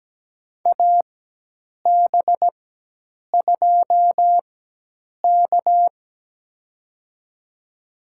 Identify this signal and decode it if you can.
Morse code, heard as a keyed tone, about 17 wpm, spaced as for 8 wpm, 700 Hz. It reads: AB2K
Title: USA Call Signs - 4 Characters - 4x Character Spacing - 17wpm